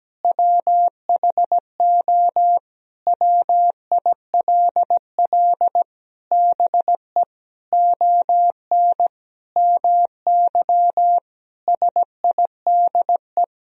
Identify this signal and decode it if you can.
Morse code, heard as a keyed tone, about 17 wpm, 700 Hz. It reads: WHO WILL BE ON MY SIDE
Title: Sentences from Top 1000 Words - Review Part 1 - 17wpm